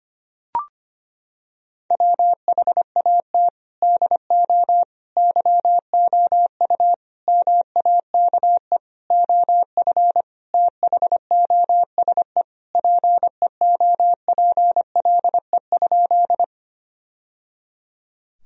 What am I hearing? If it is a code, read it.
Morse code, heard as a keyed tone, about 25 wpm, 700 Hz. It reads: WHAT DO YOU MAKE OF THOSE PEOPLE?
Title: Sentences from Top 1000 Words - Review Part 1 - 25wpm